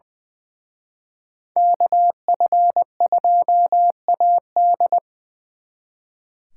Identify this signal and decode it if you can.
Morse code, heard as a keyed tone, about 22 wpm, 700 Hz. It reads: KF2AD
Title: Speed Racing - USA 2x2 Call Signs 20wpm